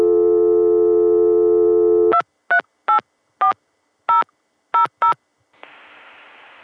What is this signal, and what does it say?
DTMF, heard as telephone touch tones, tones of about 110 ms. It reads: 2384000